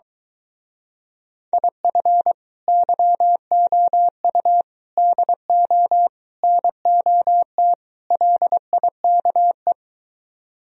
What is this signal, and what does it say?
Morse code, heard as a keyed tone, about 23 wpm, 700 Hz. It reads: IF YOU DO NOT LIKE